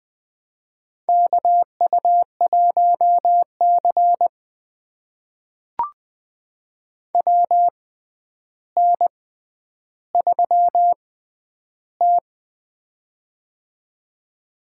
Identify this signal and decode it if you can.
Morse code, heard as a keyed tone, about 20 wpm, spaced as for 10 wpm, 700 Hz. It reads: KU1C WN3T
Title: USA Call Signs - 4 Characters - 6x Character Spacing - 20wpm